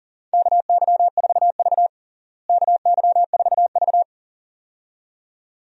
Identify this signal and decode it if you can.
Morse code, heard as a keyed tone, about 40 wpm, 700 Hz. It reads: KY4V KY4V
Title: USA Call Signs - 4 Characters - Repeated 2x - 40wpm